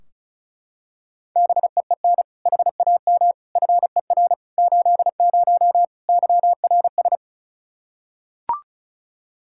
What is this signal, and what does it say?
Morse code, heard as a keyed tone, about 36 wpm, 700 Hz. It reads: BEEN HAM FER 80 YRS
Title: K1USN SST QSO Practice 35wpm